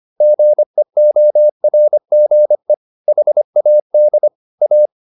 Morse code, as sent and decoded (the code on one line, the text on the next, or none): --. --- .-. --. / .... .- -.. / .-
GEORGE HAD A